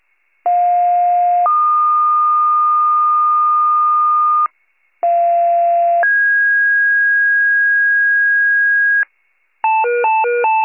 Two-tone page, heard as a one-second tone, then a three-second tone, about 710 then 1220 Hz, about 690 then 1650 Hz, with noise. It is heard from 0.5 to 9.0 s.